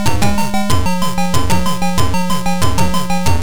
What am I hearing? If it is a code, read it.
There is no code.